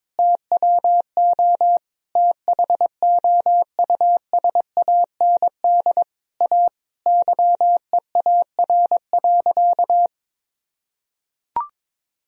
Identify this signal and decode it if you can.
Morse code, heard as a keyed tone, about 22 wpm, 700 Hz. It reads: TWO THOUSAND A YEAR.
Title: Sentences from Top 1000 Words - Review Part 3 - 22wpm